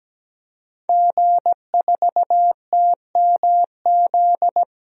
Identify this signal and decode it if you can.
Morse code, heard as a keyed tone, about 17 wpm, 700 Hz. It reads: G4TMZ
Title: DXCC-100 Call Signs - 5 Characters - 17wpm